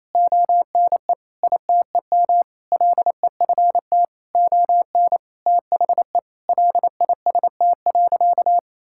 Morse code, heard as a keyed tone, about 28 wpm, 700 Hz. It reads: ONE ITEM LEFT ON THE LIST.
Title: Sentences from Top 1000 Words - Review Part 6 - 28wpm